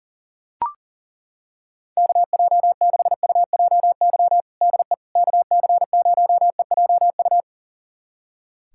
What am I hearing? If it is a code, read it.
Morse code, heard as a keyed tone, about 40 wpm, 700 Hz. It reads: KJ6UJY DE KC0EJU
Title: USA Callsign DE Callsign - 6 Characters - 40wpm